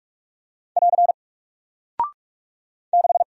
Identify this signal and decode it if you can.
Morse code, heard as a keyed tone, about 44 wpm, 700 Hz. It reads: <AR> 6
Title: Single Character (Letter:Number:Punctuation:Prosign) Alt 45wpm